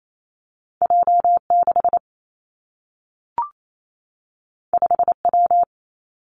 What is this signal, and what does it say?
Morse code, heard as a keyed tone, about 28 wpm, 700 Hz. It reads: J 6 5 W
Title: Two Challenging Letter-Number Combinations 28wpm